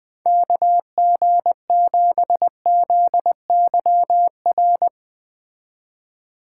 Morse code, heard as a keyed tone, about 19 wpm, 700 Hz. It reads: KG7ZYR
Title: Speed Racing - USA 2x3 Call Signs 17wpm